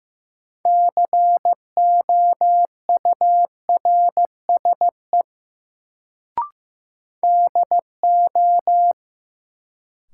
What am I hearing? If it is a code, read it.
Morse code, heard as a keyed tone, about 15 wpm, 700 Hz. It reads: COURSE DO
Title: Sets of 2 Words - New Word Drill - 15wpm